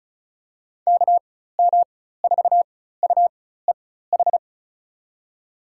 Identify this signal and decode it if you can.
Morse code, heard as a keyed tone, about 34 wpm, spaced as for 16 wpm, 700 Hz. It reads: KM4UEH
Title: USA Call Signs - 6 Characters - 4x Character Spacing - 35wpm